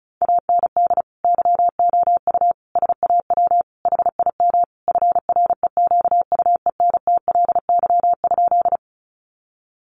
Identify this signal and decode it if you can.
Morse code, heard as a keyed tone, about 34 wpm, 700 Hz. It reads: AND YOU SAW HIM FREQUENTLY?
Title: Sentences from Top 1300 Words 35wpm